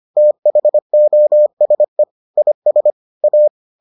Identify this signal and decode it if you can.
Morse code, heard as a keyed tone, about 25 wpm, 600 Hz. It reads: THOSE IS A